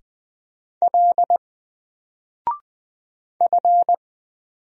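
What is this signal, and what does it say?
Morse code, heard as a keyed tone, about 20 wpm, 700 Hz. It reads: L F
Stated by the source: Single letter and number Morse drill at 20 wpm, from N4LCD.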